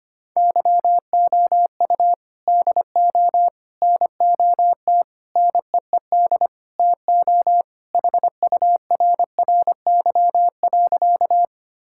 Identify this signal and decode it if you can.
Morse code, heard as a keyed tone, about 25 wpm, 700 Hz. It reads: YOU DO NOT NEED TO HURRY.
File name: practice / 25wpm / Sentences with 4-Word Phrases - 25wpm